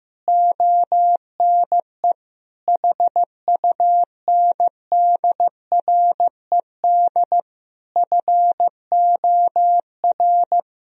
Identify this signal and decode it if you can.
Morse code, heard as a keyed tone, about 15 wpm, 700 Hz. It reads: ONE HUNDRED FOR
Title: Sentences from Top 1000 Words - Review Part 2 - 15wpm